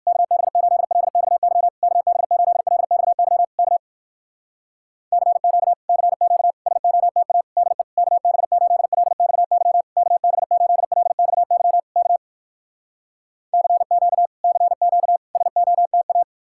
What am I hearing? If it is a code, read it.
Morse code, heard as a keyed tone, about 59 wpm, 700 Hz. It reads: KB8LXY KB8LXY K CQ CQ SOTA DE KB8LXY KB8LXY K CQ CQ SOTA